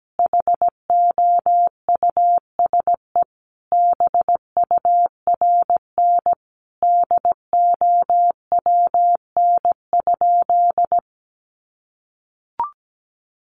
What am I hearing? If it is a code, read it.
Morse code, heard as a keyed tone, about 17 wpm, 700 Hz. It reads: HOUSE BURN DOWN?
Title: Sentences from Top 1400 Words 17wpm